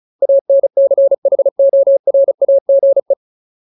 Morse code, heard as a keyed tone, about 34 wpm, 550 Hz. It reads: ANCHORAGE
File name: Ultimate Word List / 35WPM Ultimate